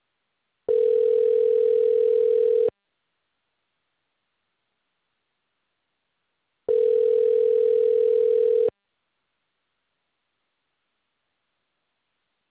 Ringback tone (North America), 440 + 480 Hz.
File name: recordings / REfb82b30d5535dd46421df3e3569cdc37